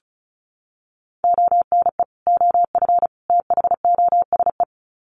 Morse code, heard as a keyed tone, about 34 wpm, 700 Hz. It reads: ONE OF THOSE
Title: Sets of 3 Words 35wpm